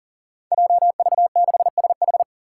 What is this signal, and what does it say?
Morse code, heard as a keyed tone, about 40 wpm, 700 Hz. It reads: JV6SH
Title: Five Challenging Letter-Number Combinations 40wpm